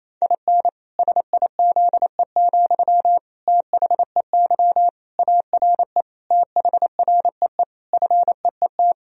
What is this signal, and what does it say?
Morse code, heard as a keyed tone, about 28 wpm, 700 Hz. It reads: IN SIZE, THEY ARE THREE FEET
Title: Sentences from Top 1000 Words - Review Part 2 - 28wpm